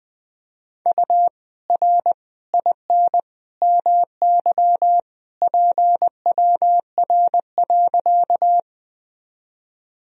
Morse code, heard as a keyed tone, about 20 wpm, 700 Hz. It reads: U R IN MY PWR.
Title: Sentences from Top 500 Words and Abbreviations 20wpm